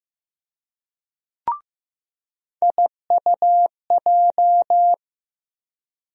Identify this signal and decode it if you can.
Morse code, heard as a keyed tone, about 15 wpm, 700 Hz. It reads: IUJ
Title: USA Call Sign Postfixes - 3 Letters 15wpm